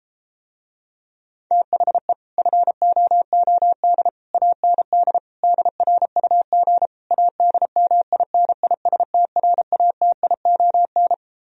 Morse code, heard as a keyed tone, about 34 wpm, 700 Hz. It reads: THE FOOD AND DRUG ADMINISTRATION